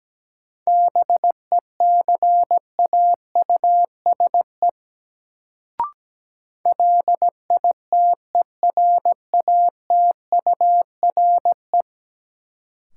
Morse code, heard as a keyed tone, about 17 wpm, 700 Hz. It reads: BECAUSE LITERATURE
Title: Sentences from Top 600 Words - New Word Drill - 17wpm